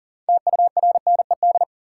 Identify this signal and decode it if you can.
Morse code, heard as a keyed tone, about 40 wpm, 700 Hz. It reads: TURNED
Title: Sets of 4 Words - New Word Drill - 40wpm